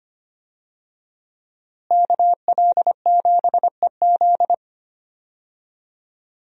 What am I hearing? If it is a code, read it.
Morse code, heard as a keyed tone, about 25 wpm, 700 Hz. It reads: KL7EZ
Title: DXCC-100 Call Signs - 5 Characters - Repeated 3x - 25wpm